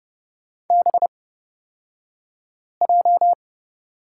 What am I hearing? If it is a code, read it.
Morse code, heard as a keyed tone, about 30 wpm, spaced as for 5 wpm, 700 Hz. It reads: BJ
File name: Lesson 36d - BT - Two Characters 30wpm